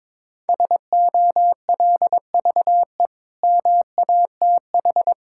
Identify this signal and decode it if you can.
Morse code, heard as a keyed tone, about 22 wpm, 700 Hz. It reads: SOLVE MATH
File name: practice / 22wpm / Easy Sentences with Hints - 22wpm